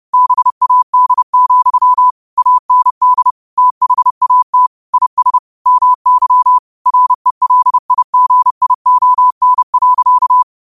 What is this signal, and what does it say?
Morse code, heard as a keyed tone, about 30 wpm, 1000 Hz. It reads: BAD, AND THAT IS MY RELIGION.